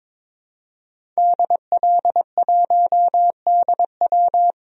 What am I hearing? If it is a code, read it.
Morse code, heard as a keyed tone, about 22 wpm, 700 Hz. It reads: DL1DW
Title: DXCC-100 Call Signs - 5 Characters - 22wpm